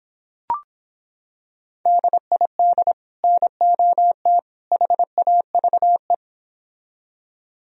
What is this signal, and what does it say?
Morse code, heard as a keyed tone, about 26 wpm, 700 Hz. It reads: DID NOT HAVE